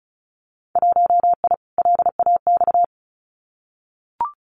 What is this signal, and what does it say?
Morse code, heard as a keyed tone, about 36 wpm, 700 Hz. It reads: 1I LAX
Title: Winter Field Day Exchanges 35wpm